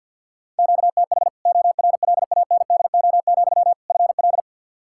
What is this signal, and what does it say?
Morse code, heard as a keyed tone, about 50 wpm, 700 Hz. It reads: QTH ORLANDO, FL
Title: Practice QSO - Exchange 1 - Beginning - 50wpm